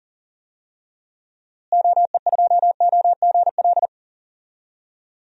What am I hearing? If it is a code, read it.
Morse code, heard as a keyed tone, about 40 wpm, 700 Hz. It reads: OE2OGL
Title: DXCC-100 Call Signs - 6 Characters - Repeated 2x - 40wpm